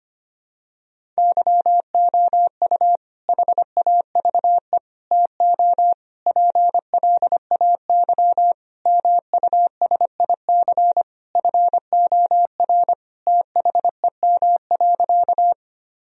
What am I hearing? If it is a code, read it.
Morse code, heard as a keyed tone, about 25 wpm, 700 Hz. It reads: YOU HAVE TO PLAY MUSIC FOR THEM.
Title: Sentences from Top 1000 Words - Review Part 1 - 25wpm